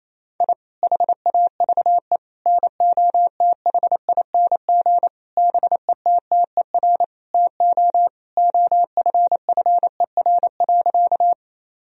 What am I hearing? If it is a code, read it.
Morse code, heard as a keyed tone, about 28 wpm, 700 Hz. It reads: I HAVE NOTHING BETTER TO OFFER.
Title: Sentences from Top 1000 Words - Review Part 4 - 28wpm